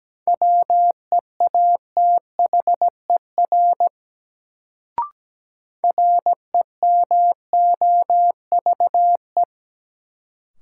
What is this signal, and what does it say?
Morse code, heard as a keyed tone, about 17 wpm, 700 Hz. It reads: WEATHER REMOVE
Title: Sentences from Top 800 Words - New Word Drill - 17wpm